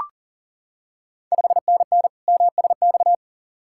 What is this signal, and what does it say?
Morse code, heard as a keyed tone, about 40 wpm, 700 Hz. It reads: 5NN MSX
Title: Ontario QSO Party Exchange 40wpm